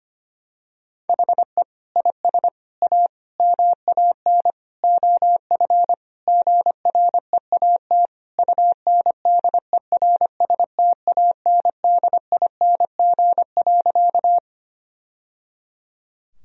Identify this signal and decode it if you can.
Morse code, heard as a keyed tone, about 25 wpm, 700 Hz. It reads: HE IS A MAN OF GREAT UNDERSTANDING.
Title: Sentences from Top 1000 Words - Review Part 3 - 25wpm